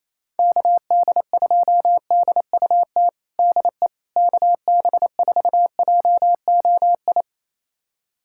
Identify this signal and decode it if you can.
Morse code, heard as a keyed tone, about 28 wpm, 700 Hz. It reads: KD2DUT DE KB4JOI
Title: USA Callsign DE Callsign - 6 Characters - 28wpm